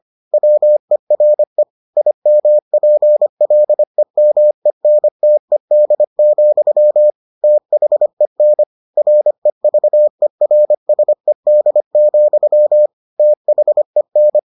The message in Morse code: WERE IMPLEMENTED, THEN REVERSED, THEN

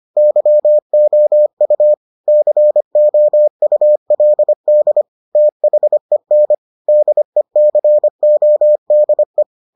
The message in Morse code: YOU COULD THEN DECODE

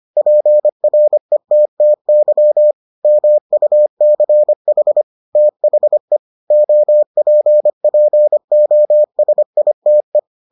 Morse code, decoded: PRETTY MUCH THE OPPOSITE

